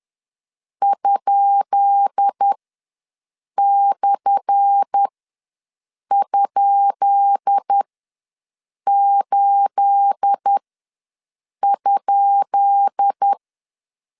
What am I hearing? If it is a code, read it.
Morse code, heard as a keyed tone, about 10 wpm, spaced as for 6 wpm, 800 Hz. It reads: ?/?8?
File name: LEZIONI / 10 SIGNS / 10B INT ws 10wpm 4134